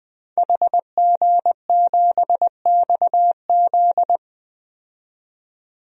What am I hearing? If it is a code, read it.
Morse code, heard as a keyed tone, about 20 wpm, 700 Hz. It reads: HG7XZ
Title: DXCC-100 Call Signs - 5 Characters - 20wpm